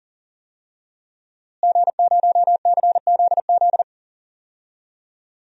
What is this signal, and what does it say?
Morse code, heard as a keyed tone, about 40 wpm, 700 Hz. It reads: G0CZZ
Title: DXCC-100 Call Signs - 5 Characters - Repeated 3x - 40wpm